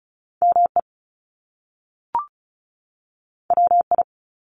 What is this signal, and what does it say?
Morse code, heard as a keyed tone, about 36 wpm, 700 Hz. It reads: ME WI
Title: US States 35wpm